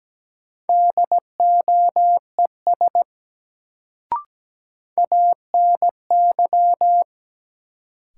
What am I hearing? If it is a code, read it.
Morse code, heard as a keyed tone, about 17 wpm, 700 Hz. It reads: DOES ANY
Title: Sentences from Top 200 Words - New Word Drill - 17wpm